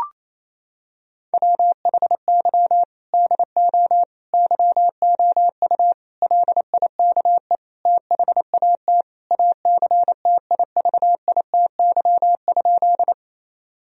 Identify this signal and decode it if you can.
Morse code, heard as a keyed tone, about 28 wpm, 700 Hz. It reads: WHY DO YOU LIKE THAT ACTIVITY?
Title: Sentences from Top 1000 Words - Review Part 4 - 28wpm